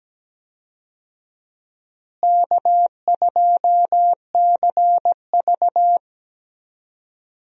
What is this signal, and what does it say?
Morse code, heard as a keyed tone, about 17 wpm, 700 Hz. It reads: K2CV
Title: USA Call Signs - 4 Characters - Repeated 3x - 17wpm